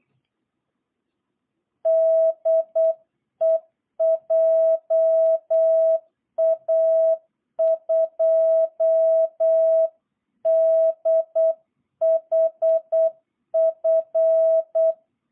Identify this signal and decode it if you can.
Morse code, heard as a keyed tone, about 8 wpm, 660 Hz. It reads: DE JA2DHF
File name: jh8cbh / 2022cw